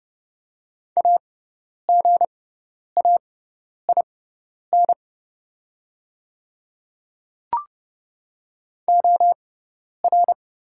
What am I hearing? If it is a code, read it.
Morse code, heard as a keyed tone, about 30 wpm, spaced as for 10 wpm, 700 Hz. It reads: AGAIN OR
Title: 6x Character Spacing - Top 300 Words - 30wpm